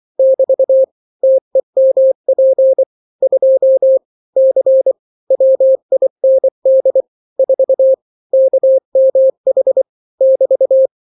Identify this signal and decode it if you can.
Morse code, heard as a keyed tone, about 24 wpm, 530 Hz. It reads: <BT> TEMP 2 C WIND 4 KMH <BT>
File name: daily / 23wpm